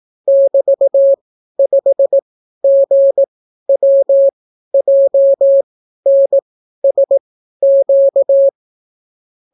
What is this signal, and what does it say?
Morse code, heard as a keyed tone, about 18 wpm, spaced as for 12 wpm, 550 Hz. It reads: <BT>5GWJNSQ